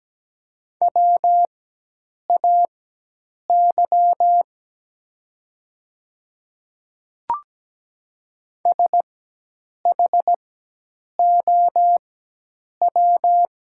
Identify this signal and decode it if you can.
Morse code, heard as a keyed tone, about 17 wpm, spaced as for 8 wpm, 700 Hz. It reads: WAY SHOW
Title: 4x Character Spacing - Top 200 Words - 17wpm